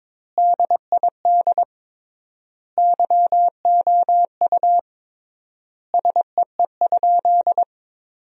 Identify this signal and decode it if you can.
Morse code, heard as a keyed tone, about 21 wpm, 700 Hz. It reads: DID YOU SEE?
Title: 3x Word-Spacing - Sentences from Top 100 Words - 22wpm